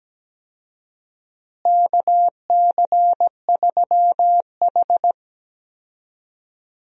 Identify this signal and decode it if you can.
Morse code, heard as a keyed tone, about 17 wpm, 700 Hz. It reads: KC3H
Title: USA Call Signs - 4 Characters - Repeated 2x - 17wpm